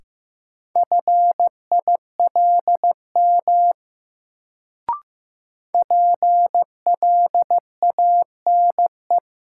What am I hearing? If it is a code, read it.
Morse code, heard as a keyed tone, about 15 wpm, 700 Hz. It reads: FILM PLANE
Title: Sentences from Top 1000 Words - Review Part 3 - New Word Drill - 15wpm